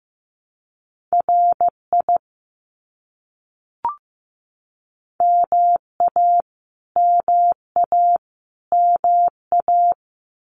Morse code, heard as a keyed tone, about 15 wpm, 700 Hz. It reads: RI MA MA MA
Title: US States Repeated 3x 15wpm